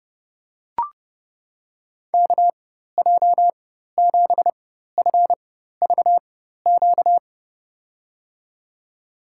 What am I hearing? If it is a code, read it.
Morse code, heard as a keyed tone, about 32 wpm, spaced as for 14 wpm, 700 Hz. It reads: KJ7FVQ
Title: USA Call Signs - 6 Characters - 4x Character Spacing - 30wpm